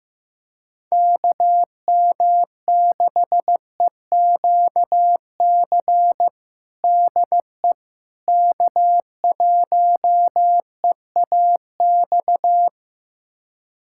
Morse code, heard as a keyed tone, about 15 wpm, 700 Hz. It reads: KM6EQC DE K1EAX
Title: USA Callsign DE Callsign 15wpm